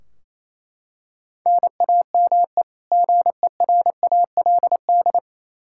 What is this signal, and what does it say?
Morse code, heard as a keyed tone, about 28 wpm, 700 Hz. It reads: NAME GERALD
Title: Practice QSO - Exchange 1 - Beginning - 28wpm